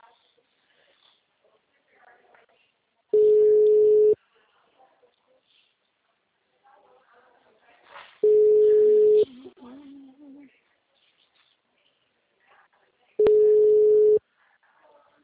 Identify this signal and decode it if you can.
Ringback tone (Europe, 425 Hz), with speech.